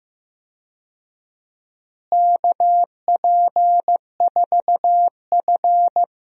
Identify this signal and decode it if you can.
Morse code, heard as a keyed tone, about 15 wpm, 700 Hz. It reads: KP4F
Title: DXCC-100 Call Signs - 4 Characters - Repeated 2x - 15wpm